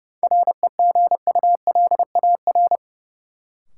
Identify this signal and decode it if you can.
Morse code, heard as a keyed tone, about 30 wpm, 700 Hz. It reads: REGULAR
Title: Sentences from Top 900 Words - New Word Drill - 30wpm